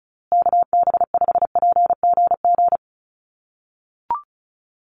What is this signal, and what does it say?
Morse code, heard as a keyed tone, about 34 wpm, 700 Hz. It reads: KB5PGG
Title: USA Call Signs Repeated 2x 35wpm